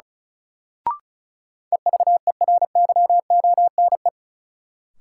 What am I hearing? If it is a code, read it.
Morse code, heard as a keyed tone, about 34 wpm, 700 Hz. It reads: EVERYONE
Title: Sentences from Top 800 Words - New Word Drill - 35wpm